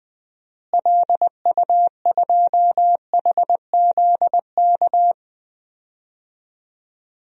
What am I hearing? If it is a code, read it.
Morse code, heard as a keyed tone, about 20 wpm, 700 Hz. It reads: LU2HZK
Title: DXCC-100 Call Signs - 6 Characters - Repeated 3x - 20wpm